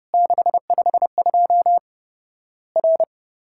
Morse code, heard as a keyed tone, about 30 wpm, 700 Hz. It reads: 652 R
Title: Worked All Europe DX Contest - QTC Exchange - 30wpm